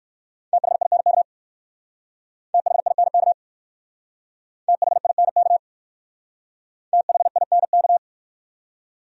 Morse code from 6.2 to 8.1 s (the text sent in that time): THINK